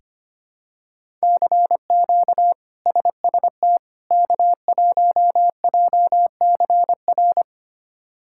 Morse code, heard as a keyed tone, about 25 wpm, 700 Hz. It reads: CQ SST K1JCR
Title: K1USN SST QSO Practice - 25wpm